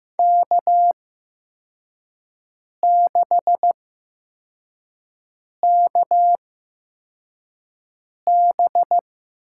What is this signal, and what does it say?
Morse code, heard as a keyed tone, about 15 wpm, spaced as for 4 wpm, 700 Hz. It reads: K6KB
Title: USA Call Signs - 8x Character Spacing - 15wpm